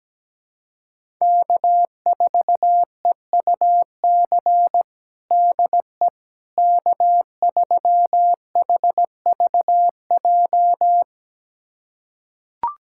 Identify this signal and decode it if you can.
Morse code, heard as a keyed tone, about 17 wpm, 700 Hz. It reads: K4EUC DE K3HVJ
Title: USA Callsign DE Callsign - 5 Characters - 17wpm